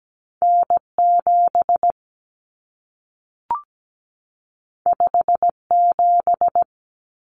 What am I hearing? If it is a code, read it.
Morse code, heard as a keyed tone, about 17 wpm, 700 Hz. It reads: N7 57
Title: Common Cut Numbers - 2-Digit 17wpm